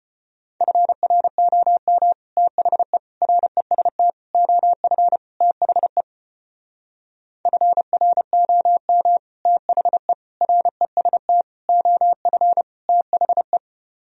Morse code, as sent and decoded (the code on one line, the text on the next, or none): ..-. .-. --- -- / - .... / .-. ... - / --- ..-. / - .... / ..-. .-. --- -- / - .... / .-. ... - / --- ..-. / - .... .
FROM THE REST OF THE FROM THE REST OF THE